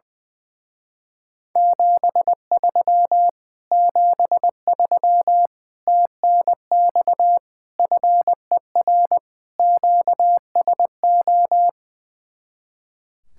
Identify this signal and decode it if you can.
Morse code, heard as a keyed tone, about 21 wpm, 700 Hz. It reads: 73 73 TNX FER QSO
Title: Speed-Racing Common QSO Phrases - 20wpm